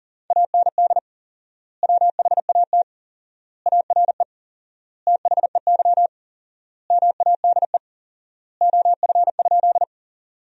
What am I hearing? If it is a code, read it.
Morse code, heard as a keyed tone, about 37 wpm, 700 Hz. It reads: AND WHAT ARE THEY MADE OF?
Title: 4x Word-Spacing - Sentences from Top 200 Words - 40wpm